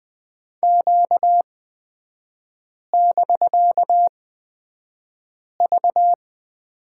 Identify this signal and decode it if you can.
Morse code, heard as a keyed tone, about 20 wpm, spaced as for 5 wpm, 700 Hz. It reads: Q<BK>V
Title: 250 Lesson 40e - BK - Three Characters 20wpm